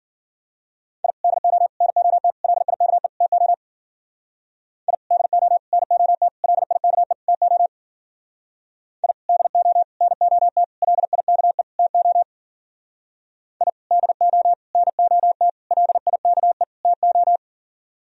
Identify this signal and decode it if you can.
Morse code, heard as a keyed tone, about 51 wpm, 700 Hz. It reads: I DO NOT LIKE TO I DO NOT LIKE TO I DO NOT LIKE TO I DO NOT LIKE TO